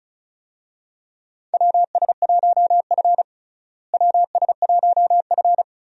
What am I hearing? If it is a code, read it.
Morse code, heard as a keyed tone, about 36 wpm, 700 Hz. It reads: WS1F WS1F
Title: USA Call Signs - 4 Characters - Repeated 2x - 35wpm